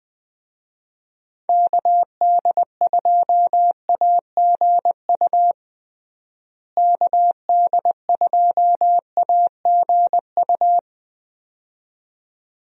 Morse code, heard as a keyed tone, about 21 wpm, 700 Hz. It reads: KD2AGU KD2AGU
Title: USA Call Signs - 6 Characters - Repeated 2x - 20wpm